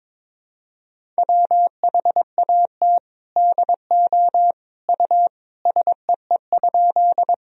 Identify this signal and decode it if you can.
Morse code, heard as a keyed tone, about 22 wpm, 700 Hz. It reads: WHAT DO U SEE?
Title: Sentences from Top 100 Words and Abbreviations 22wpm